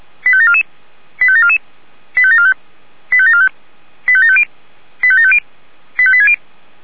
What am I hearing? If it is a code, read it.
Five-tone selective call (ZVEI), heard as quick runs of five tones, tones of about 70 ms. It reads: 85744 85744 85745 85745 85750 85750 85860